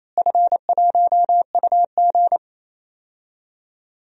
Morse code, heard as a keyed tone, about 28 wpm, 700 Hz. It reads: F1UG
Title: DXCC-100 Call Signs - 4 Characters - Repeated 3x - 28wpm